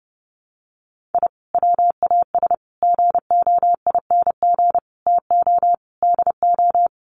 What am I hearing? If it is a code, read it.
Morse code, heard as a keyed tone, about 30 wpm, 700 Hz. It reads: I WAS GOING TO DO
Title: Sets of 5 Words Encore2 30wpm